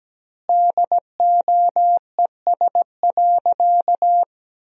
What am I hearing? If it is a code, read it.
Morse code, heard as a keyed tone, about 17 wpm, 700 Hz. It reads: DOES.
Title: 2x Word-Spacing - Sentences from Top 500 Words - 17wpm